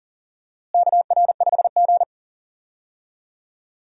Morse code, heard as a keyed tone, about 40 wpm, 700 Hz. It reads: KR5G
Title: USA Call Signs - 4 Characters - Repeated 2x - 40wpm